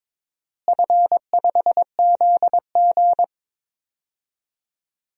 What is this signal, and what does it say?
Morse code, heard as a keyed tone, about 22 wpm, 700 Hz. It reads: F5ZG
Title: DXCC-100 Call Signs - 4 Characters - Repeated 2x - 22wpm